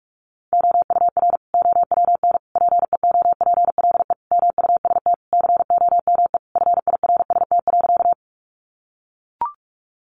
Morse code, heard as a keyed tone, about 44 wpm, 700 Hz. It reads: OUR OWN PEOPLE MUST COME FIRST.
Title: Sentences from Top 300 Words 45wpm